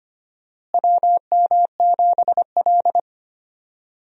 Morse code, heard as a keyed tone, about 25 wpm, 700 Hz. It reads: WM7L
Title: USA Call Signs - 4 Characters - Repeated 3x - 25wpm